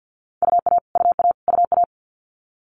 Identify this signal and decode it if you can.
Morse code, heard as a keyed tone, about 50 wpm, 690 Hz. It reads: UA UA UA
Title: Multi-letter Phonemes Repeated 3x 50wpm